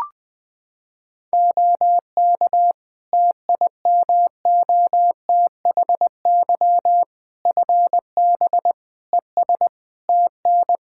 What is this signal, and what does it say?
Morse code, heard as a keyed tone, about 20 wpm, 700 Hz. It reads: OK TIMOTHY FB ES TN
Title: K1USN SST QSO Practice 20wpm